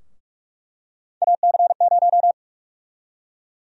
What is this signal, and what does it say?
Morse code, heard as a keyed tone, about 42 wpm, 700 Hz. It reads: AC0
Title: USA Call Sign Prefixes - 45wpm